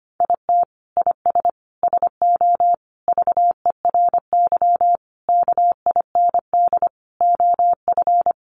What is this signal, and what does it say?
Morse code, heard as a keyed tone, about 25 wpm, 700 Hz. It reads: IT IS SO VERY KIND OF
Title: Sentences from Top 300 Words 25wpm